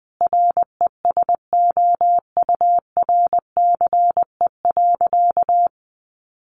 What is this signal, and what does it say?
Morse code, heard as a keyed tone, about 19 wpm, 700 Hz. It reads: RESOURCE.
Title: Sentences from Top 1300 Words 20wpm